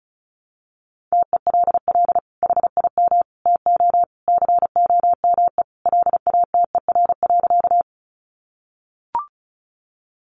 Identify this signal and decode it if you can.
Morse code, heard as a keyed tone, about 34 wpm, 700 Hz. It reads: TELL HIM TO COME LATER.
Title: Sentences from Top 700 Words 35wpm